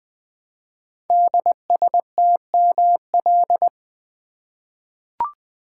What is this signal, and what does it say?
Morse code, heard as a keyed tone, about 20 wpm, 700 Hz. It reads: DSTML
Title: Code Groups - 5 Letters - 20wpm